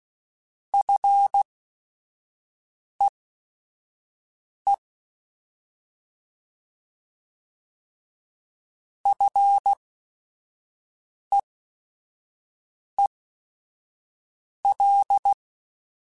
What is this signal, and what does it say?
Morse code, heard as a keyed tone, about 16 wpm, spaced as for 5 wpm, 800 Hz. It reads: FEE FEEL
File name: Short Runs / Short Runs-mp3s / SinglePrefixes @5